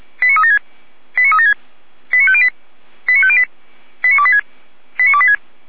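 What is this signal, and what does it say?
Five-tone selective call (ZVEI), heard as quick runs of five tones, tones of about 70 ms. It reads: 89376 89376 89508 89508 89286 89286
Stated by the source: ZVEI five-tone selective call, code 89270, received by SDR.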